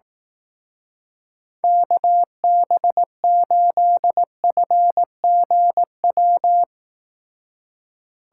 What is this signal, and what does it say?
Morse code, heard as a keyed tone, about 18 wpm, 700 Hz. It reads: KB8FGW
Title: Speed Racing - USA 2x3 Call Signs 15wpm